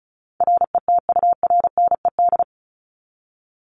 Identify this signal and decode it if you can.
Morse code, heard as a keyed tone, about 34 wpm, 700 Hz. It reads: RETURNED
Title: ED Suffix - Top 100 Words with Base Words 35wpm